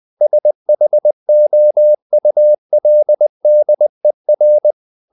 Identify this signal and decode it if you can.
Morse code, heard as a keyed tone, about 20 wpm, 600 Hz. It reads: SHOULDER
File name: the old man and the sea / the old man and the sea part 1 20wpm